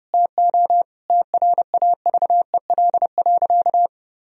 Morse code, heard as a keyed tone, about 29 wpm, 700 Hz. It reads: TO TRAVEL.
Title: Sentences from Top 1000 Words - Review Part 2 - 30wpm